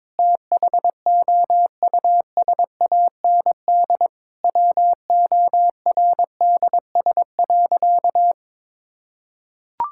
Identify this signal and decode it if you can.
Morse code, heard as a keyed tone, about 22 wpm, 700 Hz. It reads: THOUSAND WORDS.
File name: Sentences with 7-Word Phrases - 22wpm